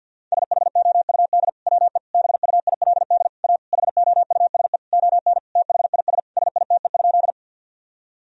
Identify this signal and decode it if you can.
Morse code, heard as a keyed tone, about 50 wpm, 700 Hz. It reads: SHOUD WE BUILD A HOUSE ON THIS SITE?
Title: Sentences from Top 1000 Words - Review Part 4 - 50wpm